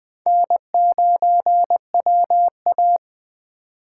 Morse code, heard as a keyed tone, about 20 wpm, 700 Hz. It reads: N9WA
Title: USA Call Signs - 4 Characters - 8x Character Spacing - 20wpm